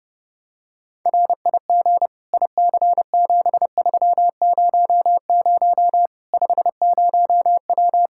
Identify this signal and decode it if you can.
Morse code, heard as a keyed tone, about 30 wpm, 700 Hz. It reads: RIG IC7300 50W
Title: Practice QSO - Exchange 2 - Info - 30wpm